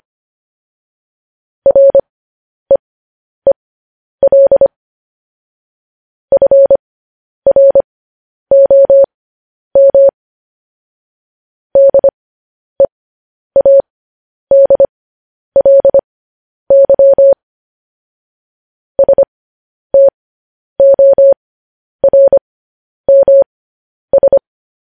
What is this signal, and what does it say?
Morse code, heard as a keyed tone, about 25 wpm, spaced as for 10 wpm, 550 Hz. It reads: REEL FROM DEADLY STORMS
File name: podcast / News headlines-1x-25x10